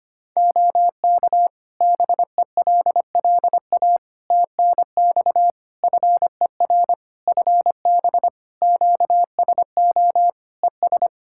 Morse code, heard as a keyed tone, about 25 wpm, 700 Hz. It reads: OK BELLA TNX FER FB QSO ES
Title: K1USN SST QSO Practice 25wpm